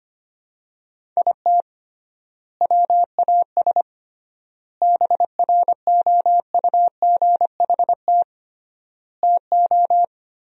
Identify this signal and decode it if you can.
Morse code, heard as a keyed tone, about 25 wpm, 700 Hz. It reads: IT WAS BROUGHT TO
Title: 3x Word-Spacing - Sentences from Top 500 Words - 25wpm